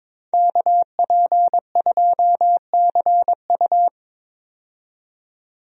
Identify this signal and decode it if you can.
Morse code, heard as a keyed tone, about 22 wpm, 700 Hz. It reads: KP2CU
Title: DXCC-100 Call Signs - 5 Characters - 22wpm